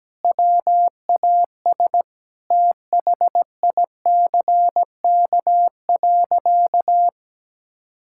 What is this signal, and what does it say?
Morse code, heard as a keyed tone, about 17 wpm, 700 Hz. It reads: WAS THICK.
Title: Sentences from Top 1600 Words 17wpm